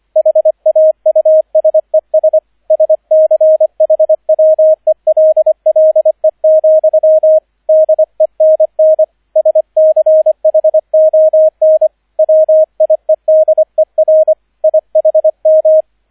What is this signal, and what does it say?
Morse code, heard as a keyed tone, about 25 wpm, 630 Hz, with radio noise. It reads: HAUSES SCHWELLE, DENN SCHON WIEDER IHM
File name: 003 Dritter Streich 25wpm